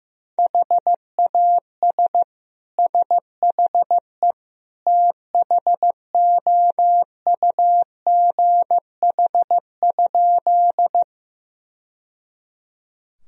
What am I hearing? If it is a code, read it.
Morse code, heard as a keyed tone, about 16 wpm, 700 Hz. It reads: HAS SHE THOUGH?
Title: Sentences from Top 1000 Words - Review Part 2 - 15wpm